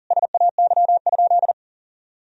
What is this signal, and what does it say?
Morse code, heard as a keyed tone, about 40 wpm, 700 Hz. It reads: SAY?
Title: Sentences from Top 1000 Words - Review Part 1 - 40wpm